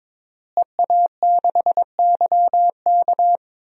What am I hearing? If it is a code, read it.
Morse code, heard as a keyed tone, about 21 wpm, 700 Hz. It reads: EA6YK
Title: DXCC-100 Call Signs - 5 Characters - Repeated 3x - 22wpm